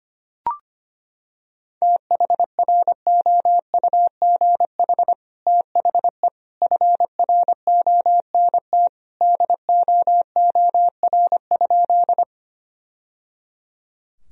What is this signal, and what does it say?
Morse code, heard as a keyed tone, about 25 wpm, 700 Hz. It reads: THROUGH THE FRONT DOOR?
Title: Sentences from Top 1000 Words - Review Part 2 - 25wpm